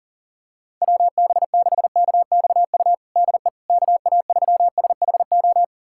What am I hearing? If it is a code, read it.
Morse code, heard as a keyed tone, about 40 wpm, 700 Hz. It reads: WB6KXU DE KA3SHO
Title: USA Callsign DE Callsign 40wpm